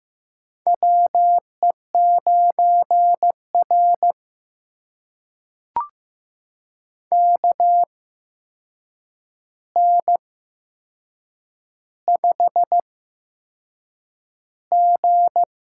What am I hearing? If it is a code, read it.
Morse code, heard as a keyed tone, about 15 wpm, spaced as for 6 wpm, 700 Hz. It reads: WE9R KN5G
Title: USA Call Signs - 4 Characters - 8x Character Spacing - 15wpm